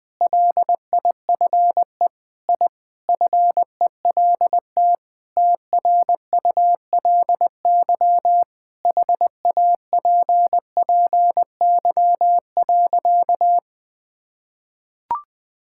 Morse code, heard as a keyed tone, about 20 wpm, 700 Hz. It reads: LIFE I FELT TRULY HAPPY.
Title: Sentences with 7-Word Phrases - 20wpm